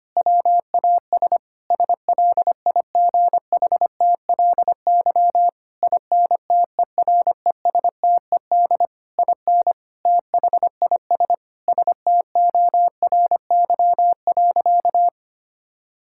Morse code, heard as a keyed tone, about 25 wpm, 700 Hz. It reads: WAS SLIGHTLY INTERESTED IN THIS STORY.
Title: Sentences from Top 1600 Words 25wpm